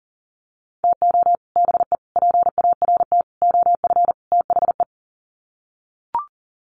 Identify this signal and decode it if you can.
Morse code, heard as a keyed tone, about 40 wpm, 700 Hz. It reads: TO BE PART OF THE
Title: Sets of 5 Words 40wpm